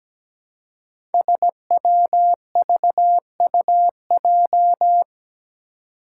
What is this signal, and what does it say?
Morse code, heard as a keyed tone, about 17 wpm, 700 Hz. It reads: SWVUJ
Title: Five Challenging Letter-Number Combinations 17wpm